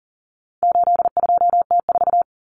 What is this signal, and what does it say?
Morse code, heard as a keyed tone, about 40 wpm, 700 Hz. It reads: 82T4
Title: Common Cut Numbers - 4-Digit 40wpm